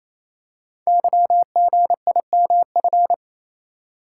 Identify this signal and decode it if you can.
Morse code, heard as a keyed tone, about 29 wpm, 700 Hz. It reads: YGIMF
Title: Code Groups - 5 Letter:Numbers - 28wpm